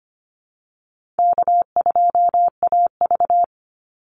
Morse code, heard as a keyed tone, about 25 wpm, 700 Hz. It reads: K2AV
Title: Easy Call Signs Repeated 3x - 25wpm